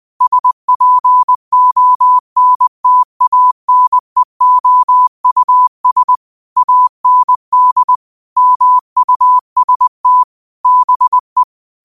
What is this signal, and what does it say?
Morse code, heard as a keyed tone, about 20 wpm, 1000 Hz. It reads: SPONTANEOUS AND MUST BE
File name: qotdfiles / 20wpm